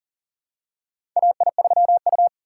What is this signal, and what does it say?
Morse code, heard as a keyed tone, about 40 wpm, 700 Hz. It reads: AI3U
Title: USA Call Signs - 6x Character Spacing - 40wpm